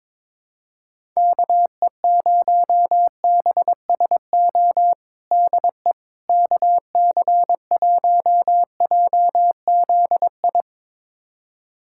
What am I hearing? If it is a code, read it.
Morse code, heard as a keyed tone, about 22 wpm, 700 Hz. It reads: KE0BSO DE KC1JZI
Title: USA Callsign DE Callsign - 6 Characters - 22wpm